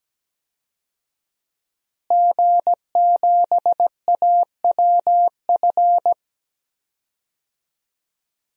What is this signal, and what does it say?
Morse code, heard as a keyed tone, about 17 wpm, 700 Hz. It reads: G7AWF
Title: DXCC-100 Call Signs - 5 Characters - Repeated 2x - 17wpm